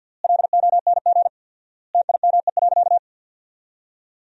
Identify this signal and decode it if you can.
Morse code, heard as a keyed tone, about 51 wpm, 700 Hz. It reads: LONG TIME.
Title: 4x Word-Spacing - Sentences from Top 400 Words - 50wpm